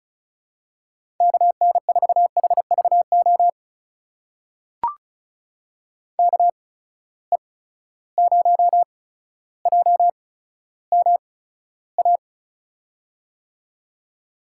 Morse code, heard as a keyed tone, about 35 wpm, spaced as for 15 wpm, 700 Hz. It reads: KN4HVO KE0JMA
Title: USA Call Signs - 6 Characters - 8x Character Spacing - 35wpm